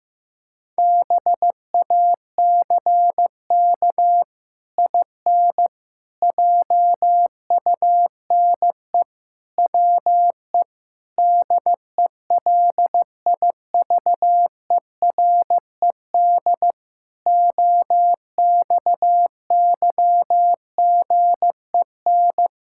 Morse code, heard as a keyed tone, about 15 wpm, 700 Hz. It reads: BACK IN JUNE WE DELIVERED OXYGEN
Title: Pangram Challenge v2 - 15wpm